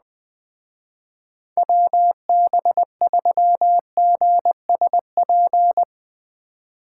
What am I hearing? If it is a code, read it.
Morse code, heard as a keyed tone, about 20 wpm, 700 Hz. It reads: WB3GSP
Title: Speed Racing - USA Call Signs 17wpm